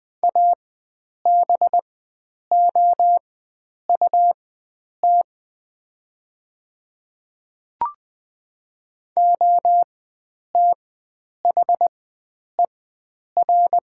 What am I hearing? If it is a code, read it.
Morse code, heard as a keyed tone, about 20 wpm, spaced as for 9 wpm, 700 Hz. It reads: ABOUT OTHER
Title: 4x Character Spacing - Top 100 Words - 20wpm